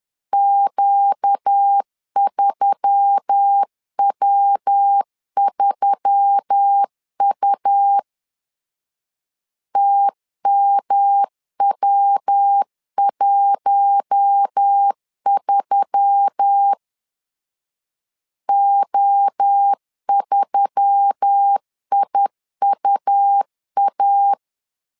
Morse code, heard as a keyed tone, about 10 wpm, 800 Hz. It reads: Q3W3U TMW13 O3IUA